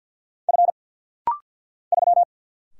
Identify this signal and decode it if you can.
Morse code, heard as a keyed tone, about 50 wpm, 690 Hz. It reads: F 3
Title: Single Letter-Number - Rapid-Fire 50wpm